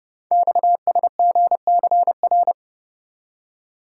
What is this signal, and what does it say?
Morse code, heard as a keyed tone, about 28 wpm, 700 Hz. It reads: XSGCR
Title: Code Groups - 5 Letters - 30wpm